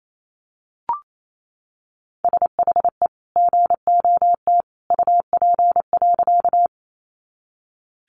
Morse code, heard as a keyed tone, about 28 wpm, 700 Hz. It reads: SHE GOT UP.
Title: Sentences from Top 300 Words 28wpm